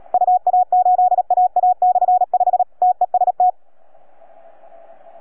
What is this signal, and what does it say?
Morse code, heard as a keyed tone, about 37 wpm, 730 Hz. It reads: UA8AA/5 TEST